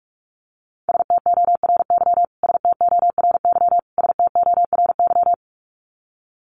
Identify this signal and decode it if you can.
Morse code, heard as a keyed tone, about 44 wpm, 700 Hz. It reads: STORY STORY STORY
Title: Top 500 Words Repeated 45wpm